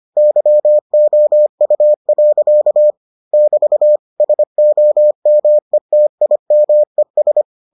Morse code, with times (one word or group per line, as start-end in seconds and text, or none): 0.2-2.9 YOU.
3.3-4.0 <BT>
4.2-7.4 SOMETIMES